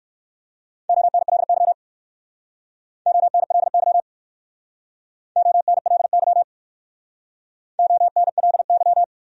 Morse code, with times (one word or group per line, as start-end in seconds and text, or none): 0.9-1.7 ONLY
3.1-4.0 ONLY
5.4-6.4 ONLY
7.8-9.0 ONLY